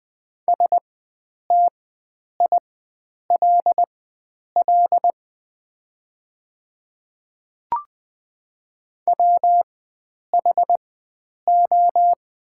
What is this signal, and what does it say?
Morse code, heard as a keyed tone, about 20 wpm, spaced as for 9 wpm, 700 Hz. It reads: STILL WHO
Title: 4x Character Spacing - Top 300 Words - 20wpm